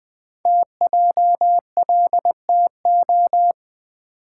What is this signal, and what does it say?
Morse code, heard as a keyed tone, about 20 wpm, 700 Hz. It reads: TJLTO
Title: Code Groups - 5 Letters - 20wpm